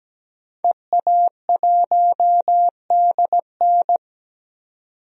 Morse code, heard as a keyed tone, about 17 wpm, 700 Hz. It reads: EA1DN
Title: DXCC-100 Call Signs - 5 Characters - 17wpm